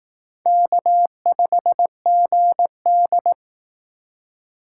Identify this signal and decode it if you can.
Morse code, heard as a keyed tone, about 18 wpm, 700 Hz. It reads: K5GD
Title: Speed Racing - USA 1x2 Call Signs 15wpm